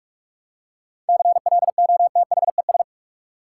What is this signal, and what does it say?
Morse code, heard as a keyed tone, about 43 wpm, 700 Hz. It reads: CLOTHES
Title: Seven Letter Words - 45wpm